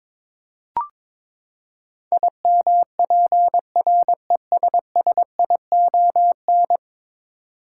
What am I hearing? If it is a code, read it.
Morse code, heard as a keyed tone, about 22 wpm, 700 Hz. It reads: IMPRESSION
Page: morse-code-ninja-practice-22wpm